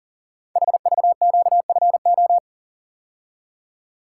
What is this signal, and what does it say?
Morse code, heard as a keyed tone, about 43 wpm, 700 Hz. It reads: HVQFO
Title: Code Groups - 5 Letters - 40wpm